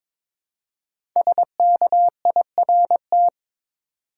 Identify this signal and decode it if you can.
Morse code, heard as a keyed tone, about 23 wpm, 700 Hz. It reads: SKIRT
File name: 5 Letter Words - First 3 Letters Given - 22wpm